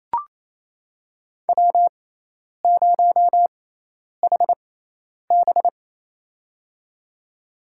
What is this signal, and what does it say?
Morse code, heard as a keyed tone, about 28 wpm, spaced as for 10 wpm, 700 Hz. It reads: W0HB
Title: USA Call Signs - 4 Characters - 6x Character Spacing - 28wpm